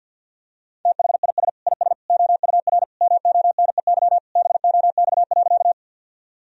Morse code, heard as a keyed tone, about 51 wpm, 700 Hz. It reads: THIS IS OUR MONEY BOX.
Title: Sentences from Top 1000 Words - Review Part 2 - 50wpm